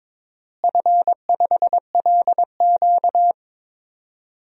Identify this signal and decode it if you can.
Morse code, heard as a keyed tone, about 21 wpm, 700 Hz. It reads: F5LQ
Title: DXCC-100 Call Signs - 4 Characters - Repeated 3x - 22wpm